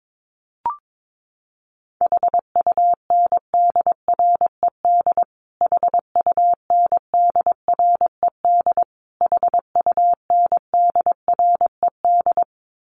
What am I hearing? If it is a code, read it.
Morse code, heard as a keyed tone, about 22 wpm, 700 Hz. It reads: HUNDRED HUNDRED HUNDRED
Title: Top 500 Words Repeated 22wpm